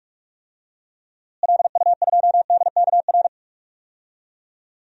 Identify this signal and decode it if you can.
Morse code, heard as a keyed tone, about 44 wpm, 700 Hz. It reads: LU2DKR
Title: DXCC-100 Call Signs - 6 Characters - Repeated 2x - 45wpm